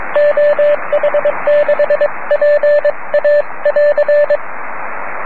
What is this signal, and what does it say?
Morse code, heard as a keyed tone, about 22 wpm, 590 Hz, with radio noise. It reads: OH6PA<AR>
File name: om3pv 70mhz / zaznam / oh1nd oh6pa